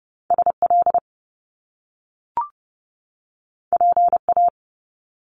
Morse code, heard as a keyed tone, about 30 wpm, 700 Hz. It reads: SL PA
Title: Top Two Letter Combinations in Words 30wpm